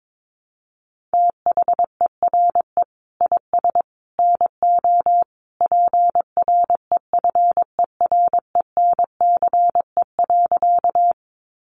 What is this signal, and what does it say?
Morse code, heard as a keyed tone, about 22 wpm, 700 Hz. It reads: THERE IS NO PREFERENCE.
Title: Sentences from Top 1500 Words 22wpm